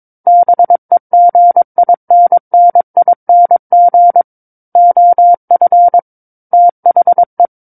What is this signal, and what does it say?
Morse code, heard as a keyed tone, about 22 wpm, 700 Hz. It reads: BEGINNING OF THE
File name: Book One / 13-Chapter 13 - 22 WPM